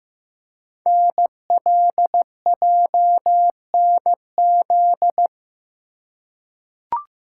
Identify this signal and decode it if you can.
Morse code, heard as a keyed tone, about 15 wpm, 700 Hz. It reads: NLJNZ
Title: Code Groups - 5 Letters - 15wpm